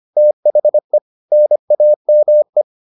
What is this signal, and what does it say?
Morse code, heard as a keyed tone, about 25 wpm, 600 Hz. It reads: THE NAME